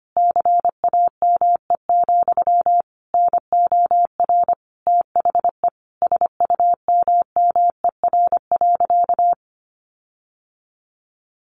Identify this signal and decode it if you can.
Morse code, heard as a keyed tone, about 25 wpm, 700 Hz. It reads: CAME, NOR THE SUMMER.
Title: Sentences from Top 1500 Words 25wpm